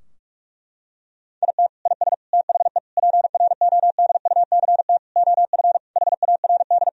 Morse code, heard as a keyed tone, about 41 wpm, 700 Hz. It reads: IT IS THE PRODUCT OF HARD
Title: Sentences from Top 1000 Words - Review Part 2 - 45wpm